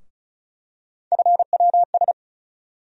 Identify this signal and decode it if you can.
Morse code, heard as a keyed tone, about 36 wpm, 700 Hz. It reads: FWS
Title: USA Call Sign Postfixes - 3 Letters 35wpm